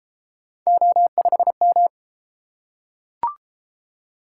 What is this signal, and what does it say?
Morse code, heard as a keyed tone, about 31 wpm, 700 Hz. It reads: O5M